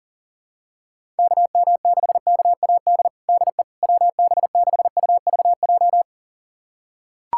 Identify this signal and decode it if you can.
Morse code, heard as a keyed tone, about 40 wpm, 700 Hz. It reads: KM6KAD DE WB6UVJ
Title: USA Callsign DE Callsign - 6 Characters - 40wpm